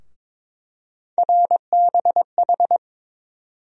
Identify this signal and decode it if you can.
Morse code, heard as a keyed tone, about 22 wpm, 700 Hz. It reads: RBH
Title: USA Call Sign Postfixes - 3 Letters 22wpm